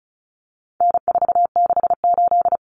Morse code, heard as a keyed tone, about 34 wpm, 700 Hz. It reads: N468
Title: Common Cut Numbers - 4-Digit 35wpm